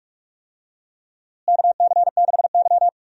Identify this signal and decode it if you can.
Morse code, heard as a keyed tone, about 46 wpm, 700 Hz. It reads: KC6Y
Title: USA Call Signs - 4 Characters - Repeated 3x - 45wpm